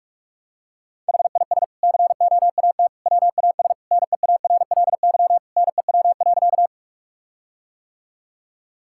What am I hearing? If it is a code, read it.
Morse code, heard as a keyed tone, about 46 wpm, 700 Hz. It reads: HIS COAT WAS NEARLY NEW.
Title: Sentences from Top 1600 Words 45wpm